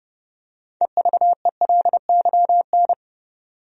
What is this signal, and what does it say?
Morse code, heard as a keyed tone, about 30 wpm, 700 Hz. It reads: EVELYN
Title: USA Common Names - 30wpm